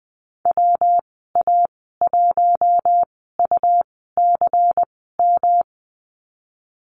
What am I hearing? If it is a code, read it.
Morse code, heard as a keyed tone, about 20 wpm, spaced as for 15 wpm, 700 Hz. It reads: WA1UCM
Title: 147 Lesson 24g - Letter y - Call Signs 20wpm